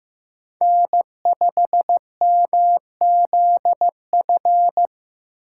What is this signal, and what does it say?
Morse code, heard as a keyed tone, about 15 wpm, 700 Hz. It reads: N5MZF
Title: USA Call Signs - 5 Characters - 3x Character Spacing - 15wpm